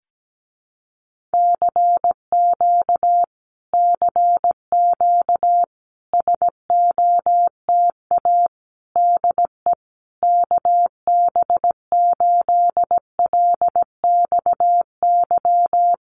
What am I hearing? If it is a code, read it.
Morse code, heard as a keyed tone, about 17 wpm, 700 Hz. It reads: CQ CQ SOTA DE KB8LXY
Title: Common QSO Phrases Encore 17wpm